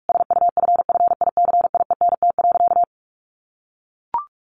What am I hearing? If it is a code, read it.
Morse code, heard as a keyed tone, about 44 wpm, 700 Hz. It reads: SUFFICIENT.
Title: Sentences from Top 1500 Words 45wpm